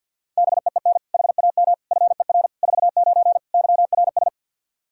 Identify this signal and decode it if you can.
Morse code, heard as a keyed tone, about 52 wpm, 700 Hz. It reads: BEEN HAM FER 49 YRS
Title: K1USN SST QSO Practice 50wpm